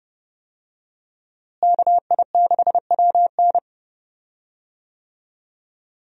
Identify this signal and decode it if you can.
Morse code, heard as a keyed tone, about 31 wpm, 700 Hz. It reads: KI6WN
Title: USA Call Signs - 5 Characters - Repeated 2x - 30wpm